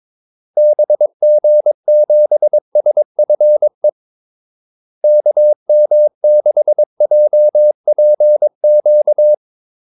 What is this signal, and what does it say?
Morse code, heard as a keyed tone, about 22 wpm, 600 Hz. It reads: BG7SFE KM6JPQ
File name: Callsigns-22wpm